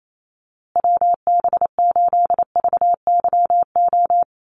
Morse code, heard as a keyed tone, about 28 wpm, 700 Hz. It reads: WB8VYO
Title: USA Call Signs Repeated 3x 28wpm